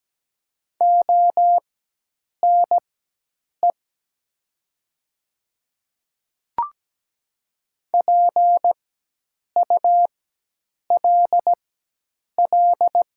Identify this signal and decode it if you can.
Morse code, heard as a keyed tone, about 17 wpm, spaced as for 8 wpm, 700 Hz. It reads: ONE PULL
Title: 4x Character Spacing - Top 500 Words - 17wpm